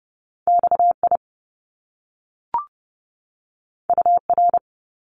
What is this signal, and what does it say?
Morse code, heard as a keyed tone, about 30 wpm, 700 Hz. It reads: XI UR
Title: Top Two Letter Combinations in Words 30wpm